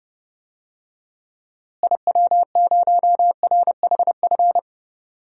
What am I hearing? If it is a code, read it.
Morse code, heard as a keyed tone, about 30 wpm, 700 Hz. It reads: IW0RHF
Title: DXCC-100 Call Signs - 6 Characters - Repeated 3x - 30wpm